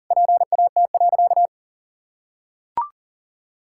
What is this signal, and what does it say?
Morse code, heard as a keyed tone, about 40 wpm, 700 Hz. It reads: PAT.
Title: Pangram Challenge v2 - 40wpm